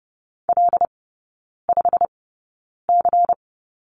Morse code, heard as a keyed tone, about 30 wpm, spaced as for 9 wpm, 700 Hz. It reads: L5C
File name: Lesson 16e - Letter c - Three Characters 30wpm